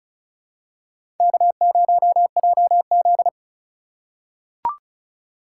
Morse code, heard as a keyed tone, about 34 wpm, 690 Hz. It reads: K0JZ
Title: USA Call Signs - 4x Character Spacing - 35wpm